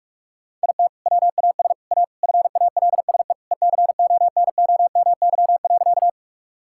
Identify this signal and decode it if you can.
Morse code, heard as a keyed tone, about 46 wpm, 700 Hz. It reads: IT WAS A FALSE ECONOMY.
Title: Sentences from Top 1600 Words 45wpm